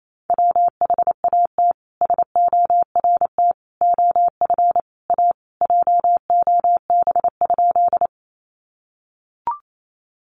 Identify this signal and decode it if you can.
Morse code, heard as a keyed tone, about 28 wpm, 700 Hz. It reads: WHAT SORT OF A JOB?
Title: Sentences from Top 1300 Words 28wpm